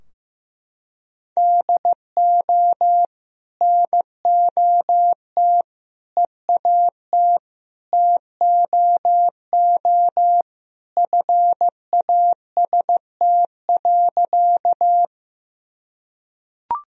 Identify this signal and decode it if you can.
Morse code, heard as a keyed tone, about 15 wpm, 700 Hz. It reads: DO NOT EAT TOO FAST.
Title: Sentences from Top 1000 Words - Review Part 2 - 15wpm